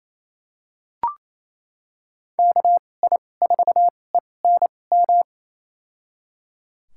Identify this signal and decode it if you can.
Morse code, heard as a keyed tone, about 28 wpm, spaced as for 20 wpm, 700 Hz. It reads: KI4ENM
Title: USA Call Signs - 6 Characters - 2x Character Spacing - 28wpm